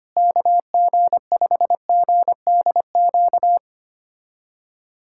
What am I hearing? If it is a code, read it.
Morse code, heard as a keyed tone, about 25 wpm, 700 Hz. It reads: KG5GDQ
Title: USA Callsign DE Callsign 25wpm